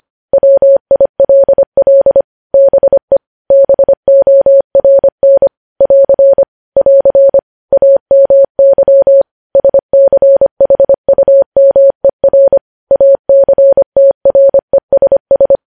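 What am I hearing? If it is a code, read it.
Morse code, heard as a keyed tone, about 25 wpm, 550 Hz. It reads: WILL BE BORN <AR> <AR> AMY SCHUMER ACTRESS
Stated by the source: News headlines sent at 25.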